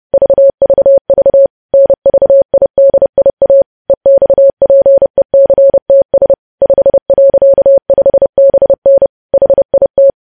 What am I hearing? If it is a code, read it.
Morse code, heard as a keyed tone, about 30 wpm, 550 Hz. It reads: VVV NVIDIA EXPECTS 5.5BN HIT
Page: news headlines in morse code at 30 wpm